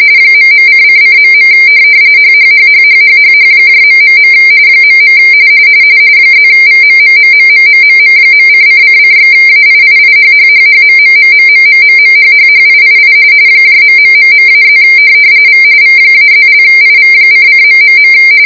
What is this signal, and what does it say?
RTTY, about 45.45 baud. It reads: ..:::.. HHHHHHHH... .. +VMMMMMMMM.::::/AIIIHHHHMMMMMMMMHHHIII ....-+ V V-+)IV::...